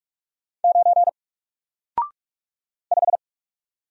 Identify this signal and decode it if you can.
Morse code, heard as a keyed tone, about 44 wpm, 700 Hz. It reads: 9 5
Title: Single Character (Letter:Number:Punctuation:Prosign) Alt 45wpm